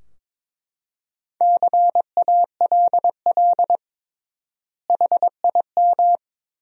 Morse code, heard as a keyed tone, about 22 wpm, 700 Hz. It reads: CALL HIM
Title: 3x Word-Spacing - Sentences from Top 100 Words - 22wpm